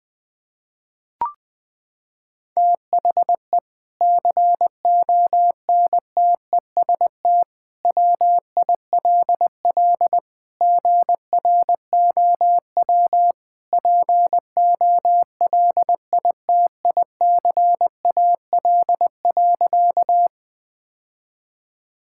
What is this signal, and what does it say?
Morse code, heard as a keyed tone, about 20 wpm, 700 Hz. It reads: THE CONTEST WILL GROW POLITICAL.
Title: Sentences from Top 1600 Words 20wpm